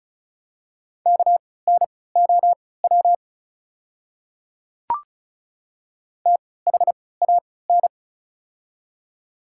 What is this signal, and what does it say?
Morse code, heard as a keyed tone, about 36 wpm, spaced as for 20 wpm, 700 Hz. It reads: KNOW THAN
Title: 3x Character Spacing - Top 100 Words - 35wpm